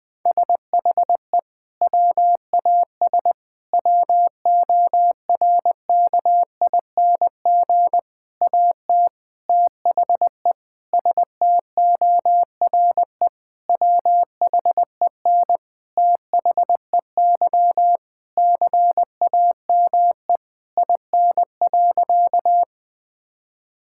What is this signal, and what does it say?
Morse code, heard as a keyed tone, about 21 wpm, 700 Hz. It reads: SHE WAS WORKING AT THE STORE WHEN THEY CAME IN.
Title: Sentences from Top 1000 Words - Review Part 5 - 20wpm